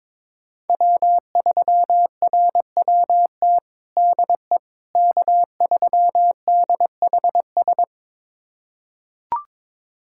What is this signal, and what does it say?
Morse code, heard as a keyed tone, about 22 wpm, 700 Hz. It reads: W3RWT DE K3DHS
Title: USA Callsign DE Callsign - 5 Characters - 22wpm